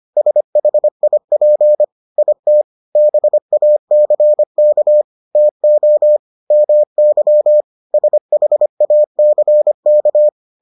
Morse code, heard as a keyed tone, about 25 wpm, 600 Hz. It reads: SHIP IT BACK TO MY SHACK